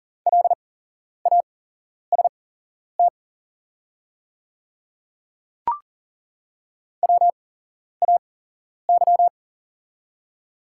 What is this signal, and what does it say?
Morse code, heard as a keyed tone, about 40 wpm, spaced as for 11 wpm, 690 Hz. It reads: LAST WAY
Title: 8x Character Spacing - Top 500 Words - 40wpm